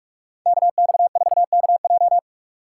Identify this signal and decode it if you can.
Morse code, heard as a keyed tone, about 46 wpm, 700 Hz. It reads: KX4KJ
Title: USA Call Signs - 5 Characters - 3x Character Spacing - 45wpm